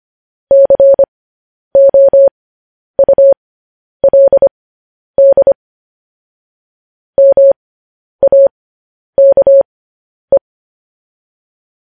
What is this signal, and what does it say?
Morse code, heard as a keyed tone, about 25 wpm, spaced as for 10 wpm, 550 Hz. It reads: COULD MAKE